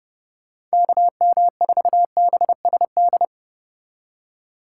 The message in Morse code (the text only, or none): KM4BSD